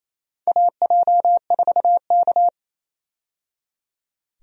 Morse code, heard as a keyed tone, about 27 wpm, 700 Hz. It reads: AJ4K
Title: USA Call Signs - 4 Characters - 28wpm